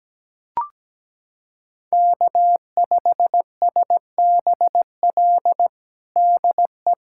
Morse code, heard as a keyed tone, about 17 wpm, 700 Hz. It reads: K5SBL DE
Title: USA Callsign DE Callsign - 5 Characters - 17wpm